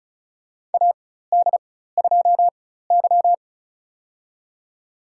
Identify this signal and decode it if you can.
Morse code, heard as a keyed tone, about 34 wpm, spaced as for 16 wpm, 700 Hz. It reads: AD2Y
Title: USA Call Signs - 4x Character Spacing - 35wpm